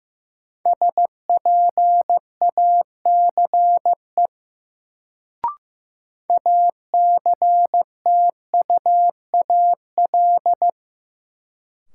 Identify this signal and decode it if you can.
Morse code, heard as a keyed tone, about 15 wpm, 700 Hz. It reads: SPACE ACTUAL
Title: Sentences from Top 1000 Words - Review Part 5 - New Word Drill - 15wpm